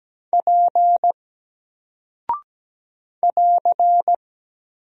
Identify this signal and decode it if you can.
Morse code, heard as a keyed tone, about 17 wpm, 700 Hz. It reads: P <AR>
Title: Single Character (Letter:Number:Punctuation:Prosign) 17wpm